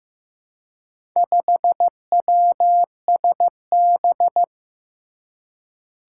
Morse code, heard as a keyed tone, about 15 wpm, 700 Hz. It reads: 5WSB
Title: Four Challenging Letter-Number Combinations 15wpm